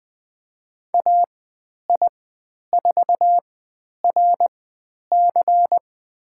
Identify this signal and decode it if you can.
Morse code, heard as a keyed tone, about 20 wpm, spaced as for 10 wpm, 700 Hz. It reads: AI4RC
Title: 109 Lesson 18g - Letter m - Call Signs 20wpm